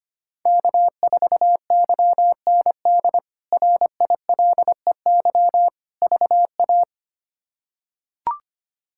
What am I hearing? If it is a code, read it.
Morse code, heard as a keyed tone, about 25 wpm, 700 Hz. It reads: K4YND RILEY VA
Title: K1USN SST QSO Practice - 25wpm